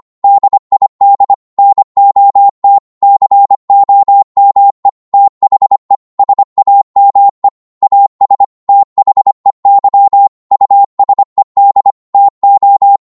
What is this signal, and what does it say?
Morse code, heard as a keyed tone, about 25 wpm, 800 Hz. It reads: DID NOT COME THE SAME AS THEY USED TO